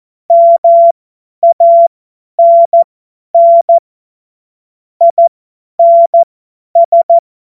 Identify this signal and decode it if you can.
Morse code, heard as a keyed tone, about 14 wpm, spaced as for 10 wpm, 680 Hz. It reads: MANN INS
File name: Witz-02-September-2025-50-F70